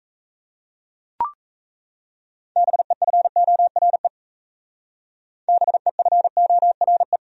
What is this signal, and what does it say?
Morse code, heard as a keyed tone, about 37 wpm, 700 Hz. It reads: BEFORE BEFORE